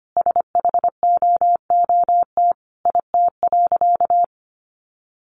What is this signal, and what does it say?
Morse code, heard as a keyed tone, about 25 wpm, 700 Hz. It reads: SHOOT IT.
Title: A Sound Of Thunder 25wpm